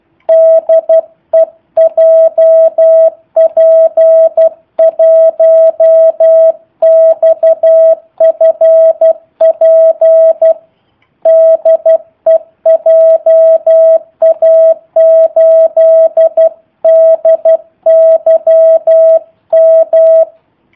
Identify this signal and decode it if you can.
Morse code, heard as a keyed tone, about 12 wpm, 650 Hz. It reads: DE JP1XFP DE JA8DYM
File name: jh8cbh / 2019cw / sp03- w